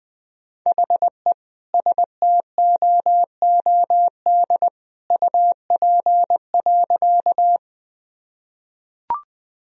Morse code, heard as a keyed tone, about 21 wpm, 700 Hz. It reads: HE STOOD UP.
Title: Sentences from Top 1000 Words - Review Part 2 - 20wpm